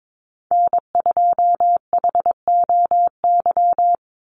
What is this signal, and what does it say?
Morse code, heard as a keyed tone, about 22 wpm, 700 Hz. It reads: N2HOY
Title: USA Call Signs 22wpm